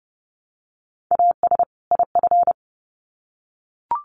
Morse code, heard as a keyed tone, about 30 wpm, 700 Hz. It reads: AS IF
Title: Sets of 2 Words 30wpm